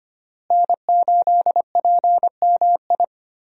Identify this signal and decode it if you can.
Morse code, heard as a keyed tone, about 25 wpm, 700 Hz. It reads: N8PMI
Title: USA Call Signs - 5 Characters - Repeated 3x - 25wpm